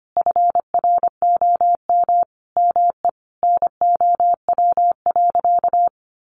Morse code, heard as a keyed tone, about 25 wpm, 700 Hz. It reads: FROM ME NOW.
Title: Sentences from Top 1300 Words 25wpm